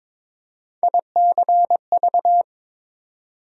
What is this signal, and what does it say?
Morse code, heard as a keyed tone, about 22 wpm, 700 Hz. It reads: ICV
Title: USA Call Sign Postfixes - 3 Letters 22wpm